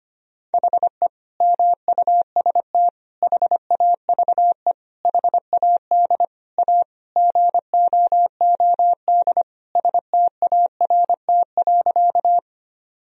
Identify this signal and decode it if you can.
Morse code, heard as a keyed tone, about 25 wpm, 700 Hz. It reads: HE MUST HAVE HAD A GOOD START.
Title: Sentences from Top 1000 Words - Review Part 1 - 25wpm